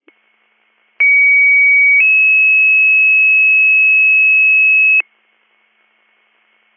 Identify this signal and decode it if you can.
Two-tone page, heard as a one-second tone, then a three-second tone, about 2250 then 2600 Hz. It is heard from 1.0 to 5.0 s.